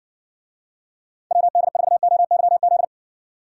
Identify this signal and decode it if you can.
Morse code, heard as a keyed tone, about 60 wpm, 700 Hz. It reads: WD4OYZ